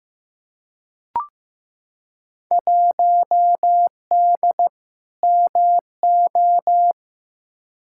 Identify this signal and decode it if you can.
Morse code, heard as a keyed tone, about 15 wpm, 700 Hz. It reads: 1D MO
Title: Field Day Exchanges - 15wpm